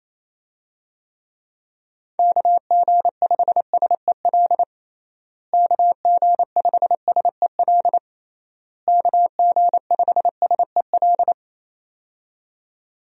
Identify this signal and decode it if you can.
Morse code, heard as a keyed tone, about 28 wpm, 700 Hz. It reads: KG5SEL KG5SEL KG5SEL
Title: USA Call Signs - 6 Characters - Repeated 3x - 28wpm